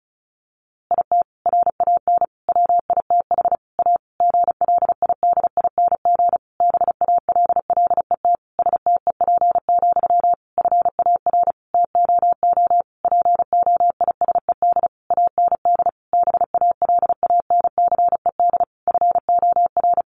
Morse code, heard as a keyed tone, about 34 wpm, 700 Hz. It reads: IT RAN WITH A GLIDING BALLET STEP, FAR TOO POISED AND BALANCED FOR
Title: A Sound Of Thunder 35wpm